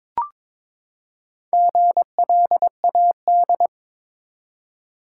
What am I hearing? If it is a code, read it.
Morse code, heard as a keyed tone, about 22 wpm, 700 Hz. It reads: GLAD
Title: Four Letter Words - 22wpm